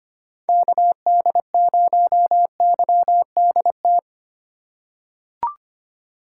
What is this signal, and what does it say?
Morse code, heard as a keyed tone, about 25 wpm, 700 Hz. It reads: KD0YDT
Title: USA Call Signs - 6 Characters - 4x Character Spacing - 25wpm